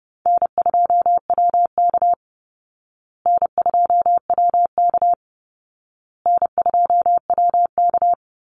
Morse code, heard as a keyed tone, about 30 wpm, 700 Hz. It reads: N2WK N2WK N2WK
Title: Call Signs Repeated 2x - 30wpm